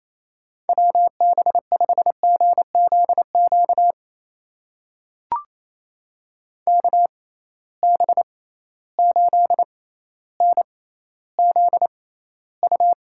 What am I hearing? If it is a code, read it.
Morse code, heard as a keyed tone, about 28 wpm, spaced as for 14 wpm, 700 Hz. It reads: WB5GZQ KB8NZU
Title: USA Call Signs - 6 Characters - 6x Character Spacing - 28wpm